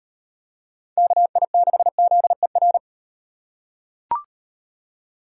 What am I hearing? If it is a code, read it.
Morse code, heard as a keyed tone, about 36 wpm, 700 Hz. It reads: KI6ZER